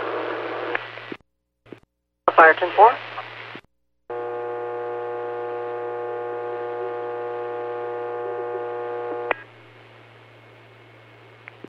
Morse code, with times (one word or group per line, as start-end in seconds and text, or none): none